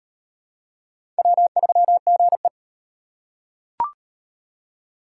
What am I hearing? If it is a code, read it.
Morse code, heard as a keyed tone, about 34 wpm, 700 Hz. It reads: W3GE